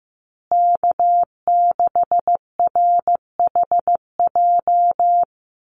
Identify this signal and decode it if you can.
Morse code, heard as a keyed tone, about 15 wpm, 700 Hz. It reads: K6RHJ
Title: USA Call Signs Repeated 3x 15wpm